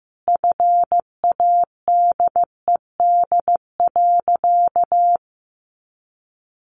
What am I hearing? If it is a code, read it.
Morse code, heard as a keyed tone, about 15 wpm, 700 Hz. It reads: FADED.
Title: A Sound Of Thunder 15wpm